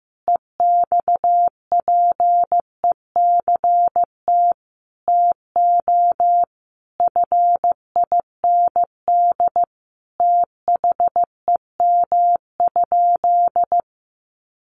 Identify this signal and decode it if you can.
Morse code, heard as a keyed tone, about 15 wpm, 700 Hz. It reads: EXPECT TO FIND THEM?
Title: Sentences from Top 1100 Words 15wpm